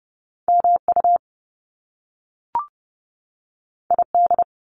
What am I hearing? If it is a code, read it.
Morse code, heard as a keyed tone, about 30 wpm, 700 Hz. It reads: MU ID
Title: Top Two Letter Combinations in Words 30wpm